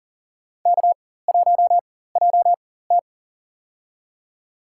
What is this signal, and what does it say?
Morse code, heard as a keyed tone, about 40 wpm, spaced as for 19 wpm, 700 Hz. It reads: K1JT
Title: USA Call Signs - 4x Character Spacing - 40wpm